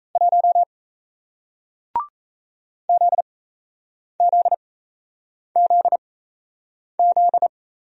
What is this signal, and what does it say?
Morse code, heard as a keyed tone, about 37 wpm, spaced as for 8 wpm, 700 Hz. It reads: 1 ZZZZ